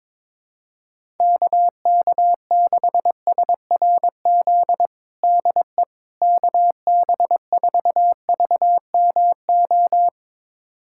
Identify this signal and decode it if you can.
Morse code, heard as a keyed tone, about 22 wpm, 700 Hz. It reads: KK6SRZ DE KB4VMO
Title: USA Callsign DE Callsign - 6 Characters - 22wpm